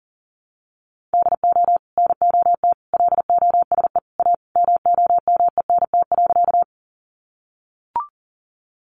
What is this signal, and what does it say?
Morse code, heard as a keyed tone, about 40 wpm, 700 Hz. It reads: DO NOT LOSE A MOMENT.
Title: Sentences from Top 1100 Words 40wpm